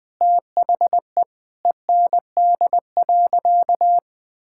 Morse code, heard as a keyed tone, about 20 wpm, 700 Hz. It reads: THE END.
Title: Sentences with 3-Word Phrases - Spoken Beginning - 20wpm